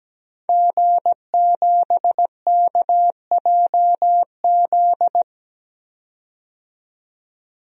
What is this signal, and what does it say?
Morse code, heard as a keyed tone, about 17 wpm, 700 Hz. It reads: G7KJZ
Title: DXCC-100 Call Signs - 5 Characters - 17wpm